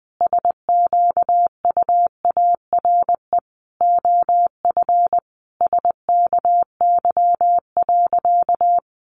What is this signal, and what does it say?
Morse code, heard as a keyed tone, about 20 wpm, 700 Hz. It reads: SQUARE OF SKY.
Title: Sentences from Top 1200 Words 20wpm